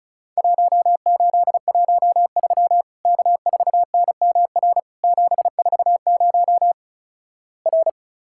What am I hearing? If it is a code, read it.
Morse code, heard as a keyed tone, about 36 wpm, 700 Hz. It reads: 1813 K4NMR 740 R
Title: Worked All Europe DX Contest - QTC Exchange - 35wpm